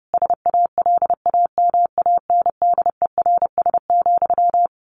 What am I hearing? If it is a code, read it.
Morse code, heard as a keyed tone, about 30 wpm, 700 Hz. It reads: SALAMANDERS,
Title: A Sound Of Thunder 30wpm